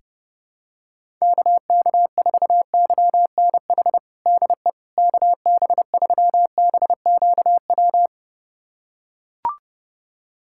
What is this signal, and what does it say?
Morse code, heard as a keyed tone, about 31 wpm, 700 Hz. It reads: KK4YNH DE KB3BQW
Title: USA Callsign DE Callsign - 6 Characters - 30wpm